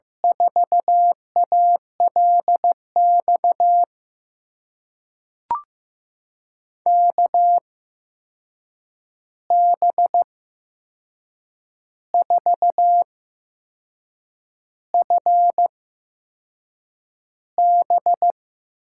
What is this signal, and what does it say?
Morse code, heard as a keyed tone, about 15 wpm, spaced as for 6 wpm, 700 Hz. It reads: 4ALX KB4FB
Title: USA Call Signs - 6 Characters - 8x Character Spacing - 15wpm